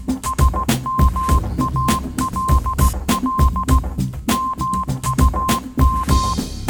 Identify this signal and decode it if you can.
Morse code, heard as a keyed tone, about 16 wpm, 1050 Hz. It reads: SMALL MUM
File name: Kirkoid / Kirkoid - Small Stuff (Morse Mix)